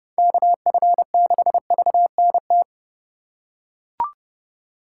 Morse code, heard as a keyed tone, about 28 wpm, 700 Hz. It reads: KF6VNT